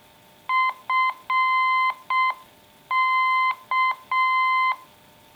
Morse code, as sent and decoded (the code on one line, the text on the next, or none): ..-. -.-
FK